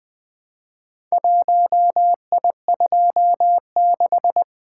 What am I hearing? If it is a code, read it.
Morse code, heard as a keyed tone, about 23 wpm, 700 Hz. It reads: 1I26
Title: Speed-Racing - Four Letter-Number Combinations - 20wpm